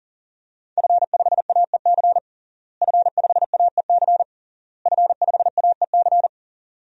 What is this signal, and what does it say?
Morse code, heard as a keyed tone, about 40 wpm, 700 Hz. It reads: F5AEC F5AEC F5AEC
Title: DXCC-100 Call Signs - 5 Characters - Repeated 3x - 40wpm